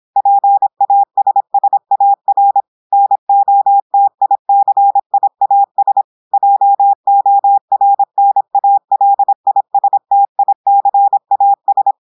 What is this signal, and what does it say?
Morse code, heard as a keyed tone, about 26 wpm, 800 Hz. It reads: PASSAR NOTICIAS JORNALISTICAS
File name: portugais texte en clair 2 vitesse 26 mots minutes